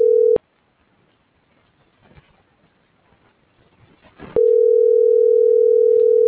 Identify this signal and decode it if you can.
Ringback tone (North America), 440 + 480 Hz.
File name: CENTRAL GENERAL / out yarit.rdz - .91601